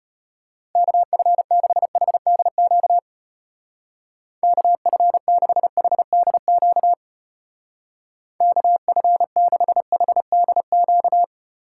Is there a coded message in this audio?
Yes: Morse code from 0.8 to 11.2 s, heard as a keyed tone, about 32 wpm, 700 Hz. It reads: KF6HDQ KF6HDQ KF6HDQ